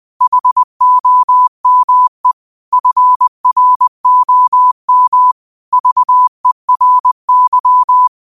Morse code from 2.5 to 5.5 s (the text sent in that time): FROM